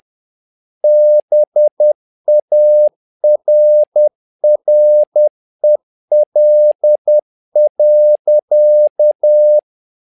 Morse code, heard as a keyed tone, about 10 wpm, 600 Hz. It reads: BARREL.
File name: learning center / tower trouble / tower trouble 10wpm